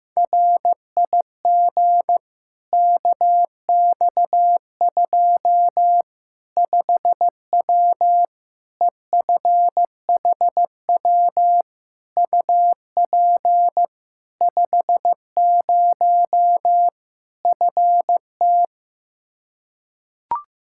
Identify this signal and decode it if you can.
Morse code, heard as a keyed tone, about 15 wpm, 700 Hz. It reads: RIG KX2 5W EFHW UP 50 FT
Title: Practice QSO - Exchange 2 - Info - 15wpm